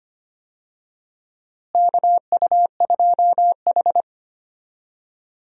Morse code, heard as a keyed tone, about 25 wpm, 700 Hz. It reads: KU2H
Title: USA Call Signs - 4 Characters - Repeated 3x - 25wpm